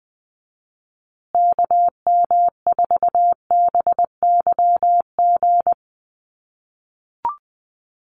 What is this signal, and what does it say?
Morse code, heard as a keyed tone, about 20 wpm, 700 Hz. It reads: KM4BYG
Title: USA Call Signs Repeated 3x 20wpm